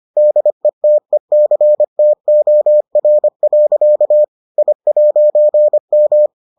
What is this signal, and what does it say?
Morse code, heard as a keyed tone, about 25 wpm, 600 Hz. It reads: DETECTOR. I'M